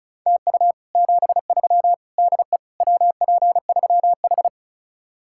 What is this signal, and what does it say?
Morse code, heard as a keyed tone, about 43 wpm, 700 Hz. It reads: TU 73 DE WP3H
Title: Speed-Racing Common QSO Phrases Encore - 35wpm